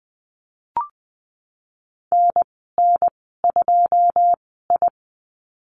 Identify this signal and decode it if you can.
Morse code, heard as a keyed tone, about 20 wpm, spaced as for 15 wpm, 700 Hz. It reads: NN2I
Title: 082 Lesson 14g - Number 2 - Call Signs 20wpm